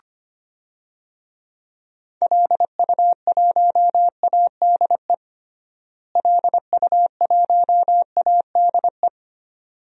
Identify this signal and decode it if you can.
Morse code, heard as a keyed tone, about 25 wpm, 700 Hz. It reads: LU1ADE LU1ADE
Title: DXCC-100 Call Signs - 6 Characters - Repeated 2x - 25wpm